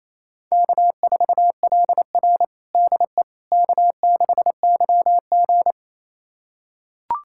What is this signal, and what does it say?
Morse code, heard as a keyed tone, about 28 wpm, 700 Hz. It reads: K4LR DE K6YG
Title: USA Callsign DE Callsign - 4 Characters - 28wpm